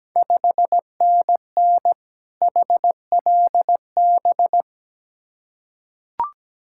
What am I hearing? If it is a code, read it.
Morse code, heard as a keyed tone, about 17 wpm, 700 Hz. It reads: 5NN HLB
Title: Ontario QSO Party Exchange 17wpm